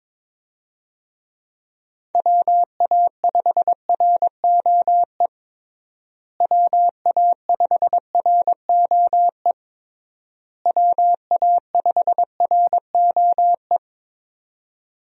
Morse code, heard as a keyed tone, about 22 wpm, 700 Hz. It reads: WA5ROE WA5ROE WA5ROE
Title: USA Call Signs - 6 Characters - Repeated 3x - 22wpm